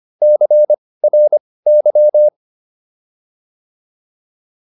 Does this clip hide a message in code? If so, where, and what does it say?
Yes: Morse code from 0.2 to 2.3 s, heard as a keyed tone, about 25 wpm, spaced as for 18 wpm, 600 Hz. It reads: CRY